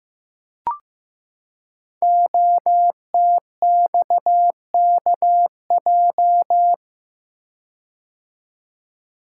Morse code, heard as a keyed tone, about 15 wpm, 700 Hz. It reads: OTXKJ
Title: Code Groups - 5 Letter:Numbers - 15wpm